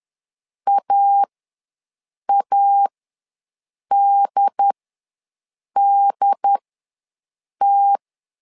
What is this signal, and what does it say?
Morse code, heard as a keyed tone, about 10 wpm, spaced as for 6 wpm, 800 Hz. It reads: AADDT
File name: LEZIONI / 2 UHDN / 2 UHD ws 10wpm 2058